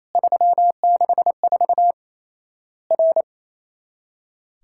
Morse code, heard as a keyed tone, about 28 wpm, 700 Hz. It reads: 364 R
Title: Worked All Europe DX Contest - QTC Exchange - 28wpm